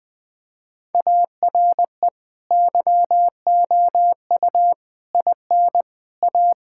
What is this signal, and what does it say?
Morse code, heard as a keyed tone, about 20 wpm, 700 Hz. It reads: ARE YOU IN A
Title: Sentences from Top 1000 Words - Review Part 6 - 20wpm